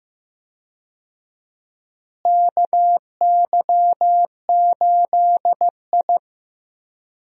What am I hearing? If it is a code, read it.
Morse code, heard as a keyed tone, about 15 wpm, 700 Hz. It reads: KY8I
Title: USA Call Signs - 4 Characters - Repeated 3x - 15wpm